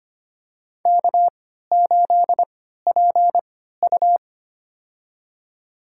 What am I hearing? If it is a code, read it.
Morse code, heard as a keyed tone, about 25 wpm, spaced as for 14 wpm, 700 Hz. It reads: K8PU
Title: USA Call Signs - 4 Characters - 3x Character Spacing - 25wpm